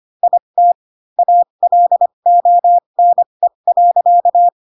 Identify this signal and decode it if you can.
Morse code, heard as a keyed tone, about 25 wpm, spaced as for 22 wpm, 700 Hz. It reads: IT ALONE.
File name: Pinocchio / Pinocchio ch3 18 25 ik1hge